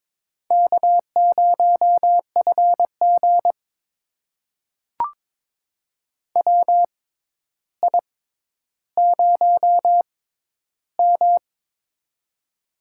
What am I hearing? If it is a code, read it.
Morse code, heard as a keyed tone, about 22 wpm, spaced as for 11 wpm, 700 Hz. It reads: K0FG WI0M
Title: USA Call Signs - 4 Characters - 6x Character Spacing - 22wpm